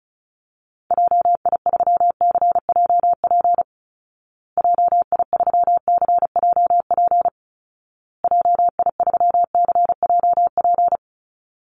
Morse code, heard as a keyed tone, about 34 wpm, 700 Hz. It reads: JI3CJP JI3CJP JI3CJP
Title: Call Signs Repeated 2x - 35wpm